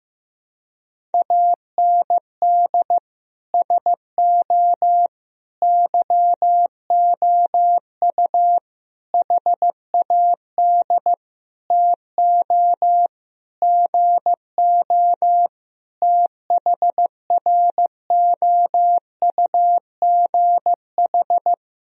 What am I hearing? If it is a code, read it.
Morse code, heard as a keyed tone, about 15 wpm, 700 Hz. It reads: AND SO YOU HAD TO GO THROUGH
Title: Sentences from Top 1000 Words - Review Part 1 - 15wpm